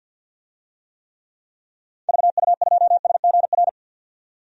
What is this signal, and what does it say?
Morse code, heard as a keyed tone, about 51 wpm, 700 Hz. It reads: VU2SGR
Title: DXCC-100 Call Signs - 6 Characters - 50wpm